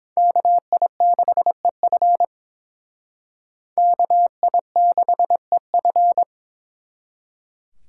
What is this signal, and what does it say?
Morse code, heard as a keyed tone, about 27 wpm, 700 Hz. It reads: KI6EF KI6EF
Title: Speed Racing - USA 2x2 Call Signs 22wpm